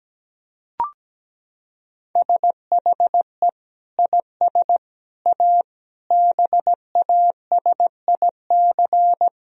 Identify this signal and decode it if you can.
Morse code, heard as a keyed tone, about 17 wpm, 700 Hz. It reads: SHE IS A BASIC
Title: Sentences from Top 1000 Words - Review Part 3 - 17wpm